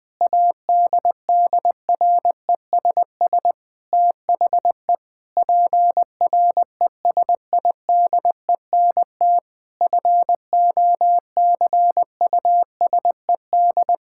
Morse code, heard as a keyed tone, about 19 wpm, 700 Hz. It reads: ADDRESS THE PRESIDENT FOCUSED
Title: Sentences with 7-Word Phrases - 20wpm